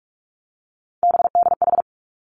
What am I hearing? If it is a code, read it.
Morse code, heard as a keyed tone, about 46 wpm, 700 Hz. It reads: BDH
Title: Three Challenging Letter-Number Combinations 45wpm